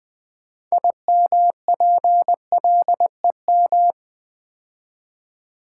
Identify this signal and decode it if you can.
Morse code, heard as a keyed tone, about 20 wpm, 700 Hz. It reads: IMPLEM
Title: Build A Word - 3 Letters At A Time - 9-Letter Words - 20wpm